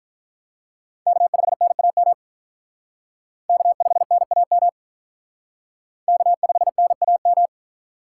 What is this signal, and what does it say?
Morse code, heard as a keyed tone, about 44 wpm, 700 Hz. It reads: K5NAM K5NAM K5NAM